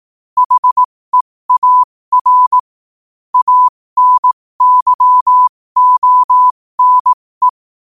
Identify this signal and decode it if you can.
Morse code, heard as a keyed tone, about 18 wpm, spaced as for 15 wpm, 1000 Hz. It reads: HEAR ANYONE
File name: qotdfiles / 15wpm